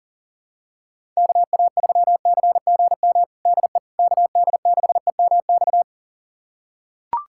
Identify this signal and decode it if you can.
Morse code, heard as a keyed tone, about 40 wpm, 700 Hz. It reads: KA3CGM DE KD6EMX
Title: USA Callsign DE Callsign 40wpm